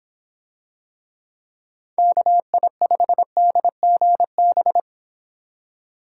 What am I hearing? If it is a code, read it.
Morse code, heard as a keyed tone, about 26 wpm, 700 Hz. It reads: KI5DGB